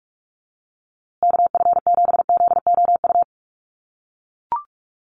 Morse code, heard as a keyed tone, about 44 wpm, 700 Hz. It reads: KF7ZOU
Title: USA Call Signs Repeated 3x 45wpm